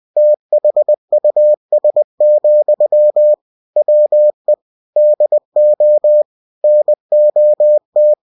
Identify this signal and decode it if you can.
Morse code, heard as a keyed tone, about 20 wpm, 600 Hz. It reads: THUS, WE DO NOT
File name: cw-books / winnie-the-pooh / winnie-20wpm-page-0122